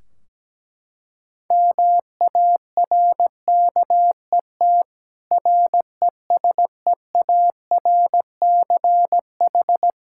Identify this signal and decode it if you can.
Morse code, heard as a keyed tone, about 17 wpm, 700 Hz. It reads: MARKET RESEARCH
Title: Practice QSO - Exchange 3 - More Info - 17wpm